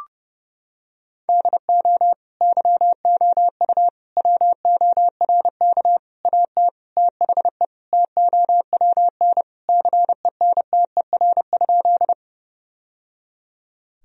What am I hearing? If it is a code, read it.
Morse code, heard as a keyed tone, about 30 wpm, 700 Hz. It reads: DO YOU WORK AT THE TOWN CENTER?
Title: Sentences from Top 1000 Words - Review Part 2 - 30wpm